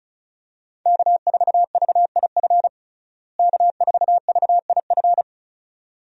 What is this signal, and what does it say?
Morse code, heard as a keyed tone, about 36 wpm, 700 Hz. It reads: K4VIF K4VIF
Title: USA Call Signs - 5 Characters - Repeated 2x - 35wpm